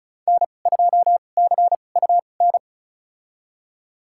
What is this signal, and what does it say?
Morse code, heard as a keyed tone, about 36 wpm, spaced as for 25 wpm, 700 Hz. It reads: N2CUN
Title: USA Call Signs - 5 Characters - 2x Character Spacing - 35wpm